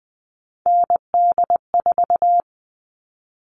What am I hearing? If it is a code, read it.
Morse code, heard as a keyed tone, about 20 wpm, 700 Hz. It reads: ND4
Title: All Cut Numbers - 3-Digit 20wpm